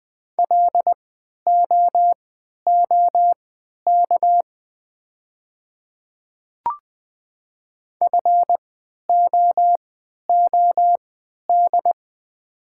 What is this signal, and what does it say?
Morse code, heard as a keyed tone, about 20 wpm, spaced as for 11 wpm, 700 Hz. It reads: LOOK FOOD
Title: 3x Character Spacing - Top 300 Words - 20wpm